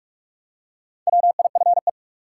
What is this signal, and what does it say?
Morse code, heard as a keyed tone, about 45 wpm, 700 Hz. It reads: WIFE
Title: Four Letter Words - 45wpm